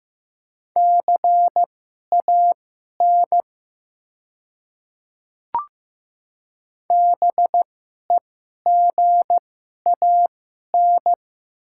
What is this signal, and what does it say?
Morse code, heard as a keyed tone, about 15 wpm, spaced as for 11 wpm, 700 Hz. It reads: CAN BEGAN
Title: 2x Character Spacing - Top 500 Words - 15wpm